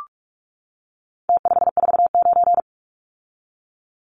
Morse code, heard as a keyed tone, about 44 wpm, 700 Hz. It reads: T549
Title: Numbers - Contest Style - 4-Digit 45wpm